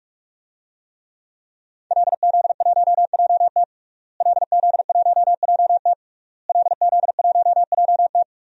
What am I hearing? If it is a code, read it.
Morse code, heard as a keyed tone, about 42 wpm, 700 Hz. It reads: LZ1JT LZ1JT LZ1JT
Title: DXCC-100 Call Signs - 5 Characters - Repeated 3x - 45wpm